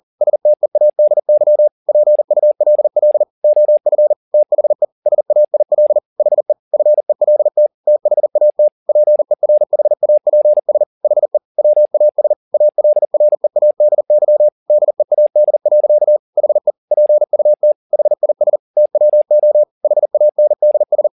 Morse code, heard as a keyed tone, about 40 wpm, 600 Hz. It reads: STEADY PULL OF THE SAIL HE FELT THAT PERHAPS HE WAS ALREADY DEAD. HE PUT HIS TWO HANDS
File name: the old man and the sea 2 40wpm